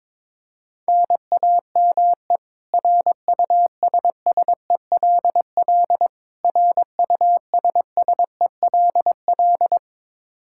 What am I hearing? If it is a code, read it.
Morse code, heard as a keyed tone, about 22 wpm, 700 Hz. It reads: NAME RUSSELL RUSSELL
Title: K1USN SST QSO Practice 22wpm